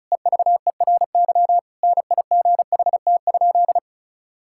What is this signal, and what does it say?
Morse code, heard as a keyed tone, about 36 wpm, 700 Hz. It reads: EVERY NIGHT?
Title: Sentences from Top 1000 Words - Review Part 5 - 35wpm